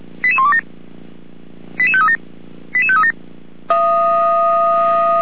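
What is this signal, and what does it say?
Five-tone selective call (ZVEI), heard as quick runs of five tones, tones of about 70 ms. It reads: 88127 88537 88537
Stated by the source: ZVEI five-tone call 88191 received on SDR.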